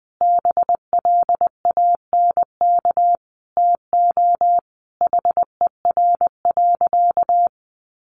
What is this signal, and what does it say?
Morse code, heard as a keyed tone, about 20 wpm, 700 Hz. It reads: BLANK TO HER.
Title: Sentences from Top 1500 Words 20wpm